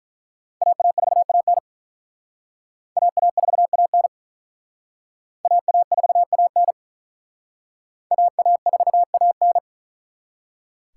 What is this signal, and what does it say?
Morse code, heard as a keyed tone, about 42 wpm, 700 Hz. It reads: AA4AN AA4AN AA4AN AA4AN
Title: Speed Racing - USA 2x2 Call Signs 35wpm